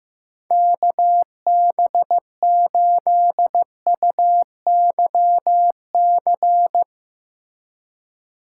Morse code, heard as a keyed tone, about 15 wpm, 700 Hz. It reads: KB8UYC
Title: USA Call Signs - 6 Characters - 2x Character Spacing - 15wpm